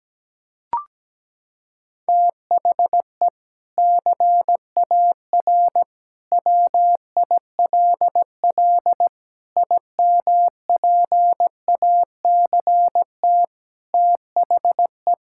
Morse code, heard as a keyed tone, about 17 wpm, 700 Hz. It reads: THE CAR WILL IMPACT THE
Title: Sentences from Top 1000 Words - Review Part 6 - 17wpm